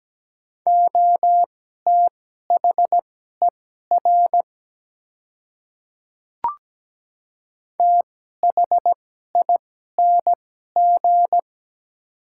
Morse code, heard as a keyed tone, about 17 wpm, spaced as for 12 wpm, 700 Hz. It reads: OTHER THING
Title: 2x Character Spacing - Top 300 Words - 17wpm